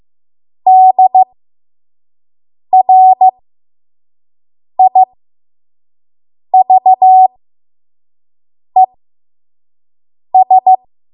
Morse code, heard as a keyed tone, about 15 wpm, spaced as for 5 wpm, 750 Hz. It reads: DRIVES
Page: 5 WPM Code Archive